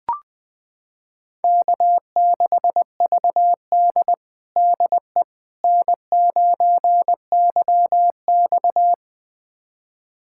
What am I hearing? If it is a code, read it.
Morse code, heard as a keyed tone, about 21 wpm, 700 Hz. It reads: K6VD DE N9YX
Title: USA Callsign DE Callsign - 4 Characters - 20wpm